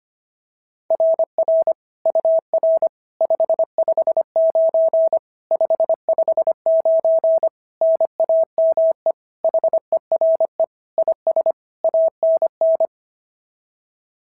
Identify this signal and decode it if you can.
Morse code, heard as a keyed tone, about 25 wpm, 660 Hz. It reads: RR UR 559 559 NAME HERE IS ANN
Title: SOTA QSO Practice 25wpm